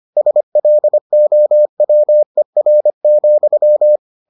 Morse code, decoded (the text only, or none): SLOWER,